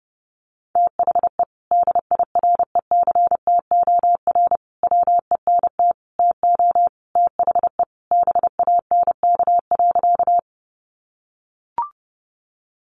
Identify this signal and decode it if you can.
Morse code, heard as a keyed tone, about 30 wpm, 700 Hz. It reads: THE DIRECTOR WENT TO THE BANK.
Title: Sentences from Top 1300 Words 30wpm